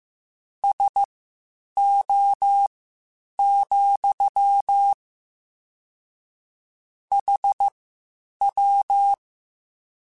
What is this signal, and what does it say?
Morse code, heard as a keyed tone, about 15 wpm, spaced as for 8 wpm, 800 Hz. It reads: SO, HW
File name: QSOs / QSOs-mp3s / WD0FBT 1@13